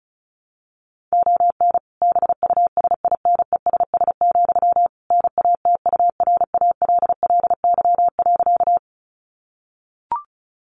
Morse code, heard as a keyed tone, about 34 wpm, 700 Hz. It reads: ON BUSINESS, NATURALLY.
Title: Sentences from Top 1300 Words 35wpm